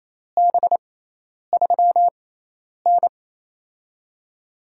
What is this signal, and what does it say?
Morse code, heard as a keyed tone, about 28 wpm, spaced as for 10 wpm, 700 Hz. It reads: B3N
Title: USA Call Signs - 6x Character Spacing - 28wpm